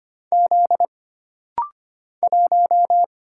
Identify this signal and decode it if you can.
Morse code, heard as a keyed tone, about 25 wpm, 700 Hz. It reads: Z 1
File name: Single Character (Letter:Number:Punctuation:Prosign) - Rapid-Fire - 25wpm